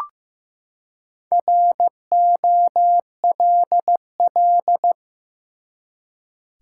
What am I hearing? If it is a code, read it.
Morse code, heard as a keyed tone, about 15 wpm, 700 Hz. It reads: ROLL
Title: Four Letter Words - 15wpm